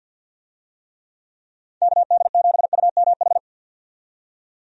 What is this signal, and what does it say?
Morse code, heard as a keyed tone, about 53 wpm, 700 Hz. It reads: KD7UMH
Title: USA Call Signs - 6 Characters - 50wpm